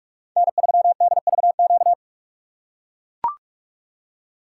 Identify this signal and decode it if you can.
Morse code, heard as a keyed tone, about 44 wpm, 690 Hz. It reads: N3DVQ
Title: USA Call Signs - 5 Characters - 2x Character Spacing - 45wpm